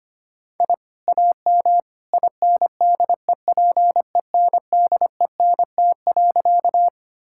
Morse code, heard as a keyed tone, about 25 wpm, 700 Hz. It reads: I AM INDEPENDENT.
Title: Sentences from Top 1000 Words - Review Part 6 - 25wpm